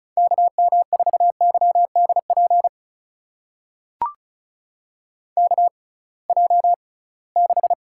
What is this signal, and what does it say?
Morse code, heard as a keyed tone, about 36 wpm, spaced as for 23 wpm, 700 Hz. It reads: KM4YDP KJ6
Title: USA Call Signs - 6 Characters - 6x Character Spacing - 35wpm